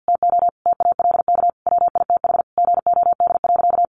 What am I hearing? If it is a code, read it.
Morse code, heard as a keyed tone, about 50 wpm, 700 Hz. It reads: TO TALK WITH GOD.
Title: A Sound Of Thunder 50wpm